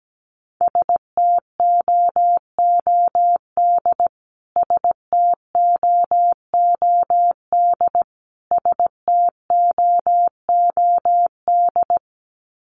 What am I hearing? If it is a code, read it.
Morse code, heard as a keyed tone, about 17 wpm, 700 Hz. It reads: STOOD STOOD STOOD
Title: Top 500 Words Repeated 17wpm